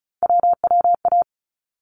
Morse code, heard as a keyed tone, about 34 wpm, 700 Hz. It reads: WWA
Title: ARRL Sections 35wpm